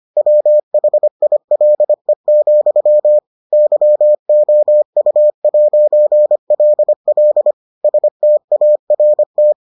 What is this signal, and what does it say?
Morse code, heard as a keyed tone, about 25 wpm, 600 Hz. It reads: WHILE, YOU'LL START